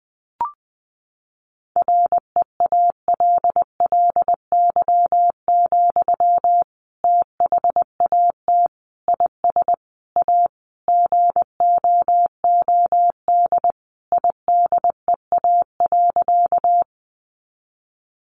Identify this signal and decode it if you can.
Morse code, heard as a keyed tone, about 20 wpm, 700 Hz. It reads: REALLY, THAT IS A GOOD IDEA.
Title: Sentences from Top 600 Words 20wpm